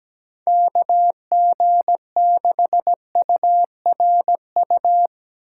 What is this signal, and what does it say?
Morse code, heard as a keyed tone, about 17 wpm, 700 Hz. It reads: KG6URU
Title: USA Call Signs - 6 Characters - 8x Character Spacing - 17wpm